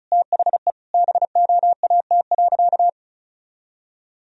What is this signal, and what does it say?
Morse code, heard as a keyed tone, about 36 wpm, 700 Hz. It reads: THE BOAT.
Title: Sentences from Top 1000 Words - Review Part 4 - 35wpm